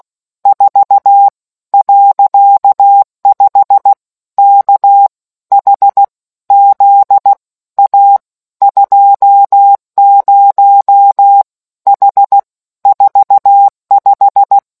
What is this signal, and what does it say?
Morse code, heard as a keyed tone, about 16 wpm, spaced as for 13 wpm, 800 Hz. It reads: 4.5 KHZ A 20 H 45